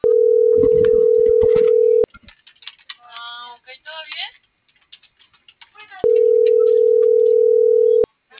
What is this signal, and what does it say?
Ringback tone (North America), 440 + 480 Hz.